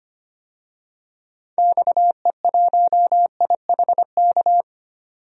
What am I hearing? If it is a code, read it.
Morse code, heard as a keyed tone, about 25 wpm, 700 Hz. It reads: XE1IHK
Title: DXCC-100 Call Signs - 6 Characters - 25wpm